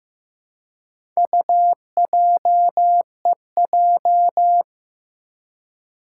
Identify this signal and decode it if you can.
Morse code, heard as a keyed tone, about 15 wpm, 700 Hz. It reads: UJEJ
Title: Four Challenging Letter-Number Combinations 15wpm